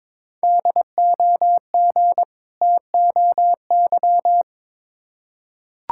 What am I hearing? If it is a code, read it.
Morse code, heard as a keyed tone, about 22 wpm, 700 Hz. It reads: DOG TOY
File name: Difficult Words with Hints - 22wpm